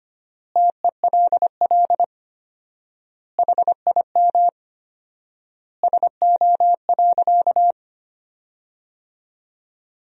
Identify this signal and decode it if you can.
Morse code, heard as a keyed tone, about 25 wpm, 700 Hz. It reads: TELL HIM SO.
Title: 4x Word-Spacing - Sentences from Top 200 Words - 25wpm